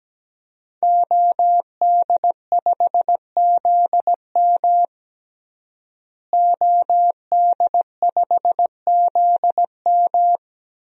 Morse code, heard as a keyed tone, about 17 wpm, 700 Hz. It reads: OD5ZM OD5ZM
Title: DXCC-100 Call Signs - 5 Characters - Repeated 3x - 17wpm